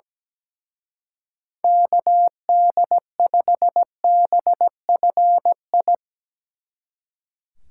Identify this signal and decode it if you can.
Morse code, heard as a keyed tone, about 18 wpm, 700 Hz. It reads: KD5BFI
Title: Speed Racing - USA 2x3 Call Signs 17wpm